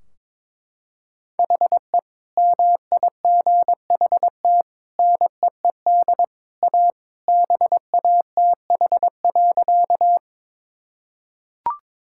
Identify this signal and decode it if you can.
Morse code, heard as a keyed tone, about 22 wpm, 700 Hz. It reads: HE MIGHT NEED A BATH.
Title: Sentences from Top 1600 Words 22wpm